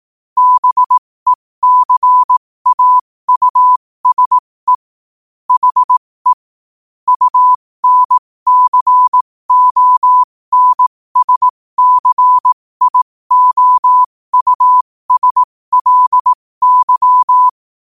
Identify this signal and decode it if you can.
Morse code, heard as a keyed tone, about 18 wpm, spaced as for 15 wpm, 1000 Hz. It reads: BECAUSE HE UNCONSCIOUSLY
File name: qotdfiles / 15wpm